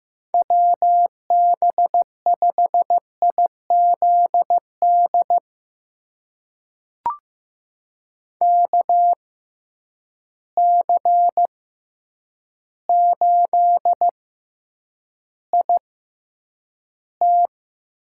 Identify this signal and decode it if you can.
Morse code, heard as a keyed tone, about 15 wpm, spaced as for 8 wpm, 700 Hz. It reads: WB5IZD KC8IT
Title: USA Call Signs - 6x Character Spacing - 15wpm